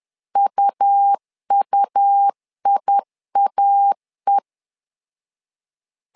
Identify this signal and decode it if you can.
Morse code, heard as a keyed tone, about 10 wpm, 800 Hz. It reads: UUIAE
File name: LEZIONI / 2 UHDN / 2 UH 10wpm 1714